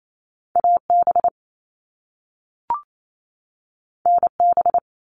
Morse code, heard as a keyed tone, about 28 wpm, 700 Hz. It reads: AB NB
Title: Canadian Provinces and Territories 28wpm